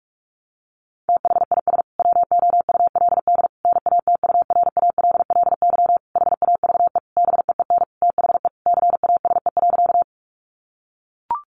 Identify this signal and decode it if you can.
Morse code, heard as a keyed tone, about 44 wpm, 700 Hz. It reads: THIS WOULD NATURALLY HAVE BEEN THE CASE.
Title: Sentences from Top 1300 Words 45wpm